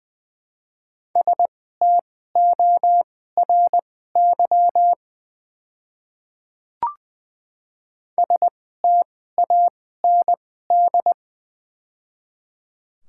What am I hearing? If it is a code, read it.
Morse code, heard as a keyed tone, about 20 wpm, spaced as for 14 wpm, 700 Hz. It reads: STORY STAND
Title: 2x Character Spacing - Top 500 Words - 20wpm